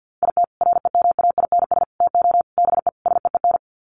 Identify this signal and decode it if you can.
Morse code, heard as a keyed tone, about 50 wpm, 700 Hz. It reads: IT REMAINS TO BE SEEN
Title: Sets of 5 Words Encore 50wpm